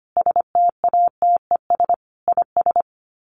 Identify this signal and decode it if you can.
Morse code, heard as a keyed tone, about 25 wpm, 700 Hz. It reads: STATES IS
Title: Sets of 5 Words Encore2 25wpm